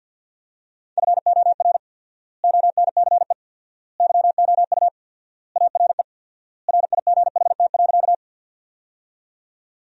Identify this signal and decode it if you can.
Morse code, heard as a keyed tone, about 52 wpm, 700 Hz. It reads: FOR ONCE YOU ARE RIGHT.
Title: 4x Word-Spacing - Sentences from Top 300 Words - 50wpm